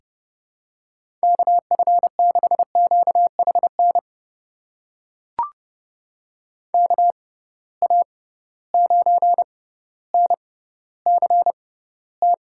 Morse code, heard as a keyed tone, about 30 wpm, spaced as for 15 wpm, 700 Hz. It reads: KF6QHN KA9NCT
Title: USA Call Signs - 6 Characters - 6x Character Spacing - 30wpm